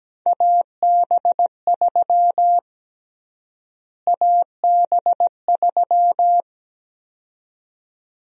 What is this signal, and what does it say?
Morse code, heard as a keyed tone, about 17 wpm, 700 Hz. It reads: AB3 AB3
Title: USA Call Sign Prefixes - Repeated 2x - 17wpm